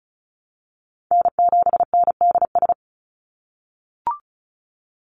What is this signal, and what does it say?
Morse code, heard as a keyed tone, about 34 wpm, 700 Hz. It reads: N7NDS
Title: USA Call Signs 35wpm